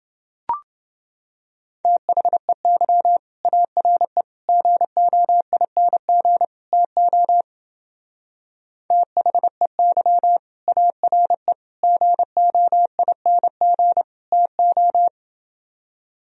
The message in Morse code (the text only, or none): THEY ARE GOING TO THEY ARE GOING TO